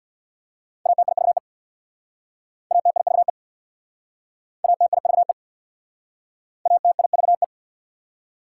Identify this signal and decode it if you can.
Morse code, heard as a keyed tone, about 68 wpm, 690 Hz. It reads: ATIVE ATIVE ATIVE ATIVE